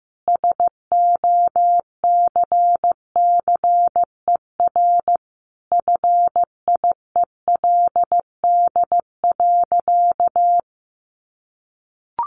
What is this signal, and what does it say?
Morse code, heard as a keyed tone, about 15 wpm, 700 Hz. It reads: SOCCER FIELD.
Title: Easy Phrases 15wpm